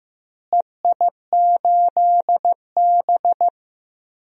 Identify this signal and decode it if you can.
Morse code, heard as a keyed tone, about 16 wpm, 700 Hz. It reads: EI8B
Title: DXCC-100 Call Signs - 4 Characters - Repeated 3x - 15wpm